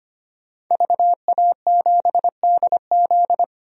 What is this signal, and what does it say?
Morse code, heard as a keyed tone, about 25 wpm, 700 Hz. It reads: VA7DZ
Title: DXCC-100 Call Signs - 5 Characters - 25wpm